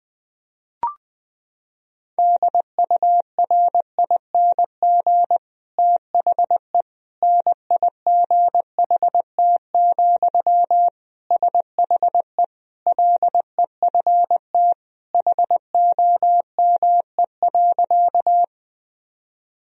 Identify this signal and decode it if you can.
Morse code, heard as a keyed tone, about 21 wpm, 700 Hz. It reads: DURING THE NIGHT, SHE LEFT HOME.
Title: Sentences from Top 1000 Words - Review Part 2 - 20wpm